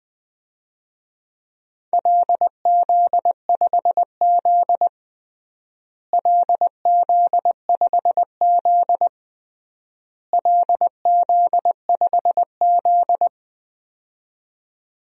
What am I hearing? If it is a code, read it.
Morse code, heard as a keyed tone, about 20 wpm, 700 Hz. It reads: LZ5Z LZ5Z LZ5Z
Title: DXCC-100 Call Signs - 4 Characters - Repeated 3x - 20wpm